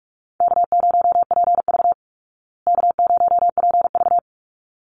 Morse code, heard as a keyed tone, about 44 wpm, 700 Hz. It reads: K0PV K0PV
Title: Easy Call Signs Repeated 2x - 45wpm